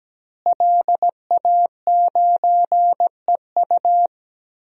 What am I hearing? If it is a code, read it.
Morse code, heard as a keyed tone, about 17 wpm, 700 Hz. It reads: LA9EU
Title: DXCC-100 Call Signs - 5 Characters - Repeated 2x - 17wpm